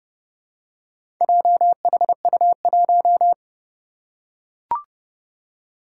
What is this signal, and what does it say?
Morse code, heard as a keyed tone, about 29 wpm, 700 Hz. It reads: JHU1
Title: Four Challenging Letter-Number Combinations 30wpm